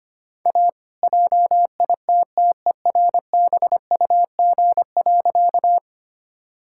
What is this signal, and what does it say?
Morse code, heard as a keyed tone, about 25 wpm, 700 Hz. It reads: A JITTERBUG.
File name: practice / 25wpm / All-Age Appropriate Jokes - 25wpm